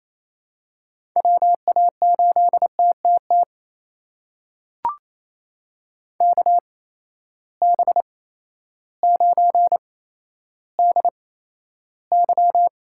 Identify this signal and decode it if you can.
Morse code, heard as a keyed tone, about 28 wpm, spaced as for 13 wpm, 700 Hz. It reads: WA8TTT KB9DY
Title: USA Call Signs - 6 Characters - 8x Character Spacing - 28wpm